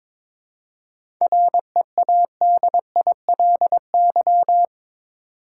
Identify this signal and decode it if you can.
Morse code, heard as a keyed tone, about 22 wpm, 700 Hz. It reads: READILY
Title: Seven Letter Words - 22wpm